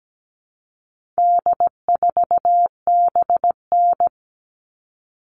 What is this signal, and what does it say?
Morse code, heard as a keyed tone, about 17 wpm, 700 Hz. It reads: D4BN
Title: All Cut Numbers - 4-Digit 17wpm